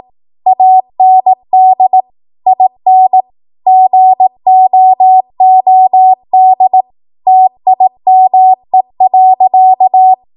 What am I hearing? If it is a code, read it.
Morse code, heard as a keyed tone, about 18 wpm, 750 Hz. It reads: AND IN GOOD TIME.
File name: Morse / 18WPM